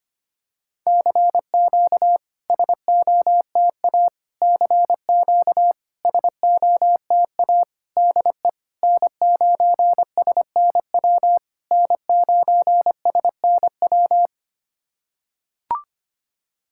Morse code, heard as a keyed tone, about 25 wpm, 700 Hz. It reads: CQ SOTA CQ SOTA DE N9SNW N9SNW
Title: SOTA QSO Practice 25wpm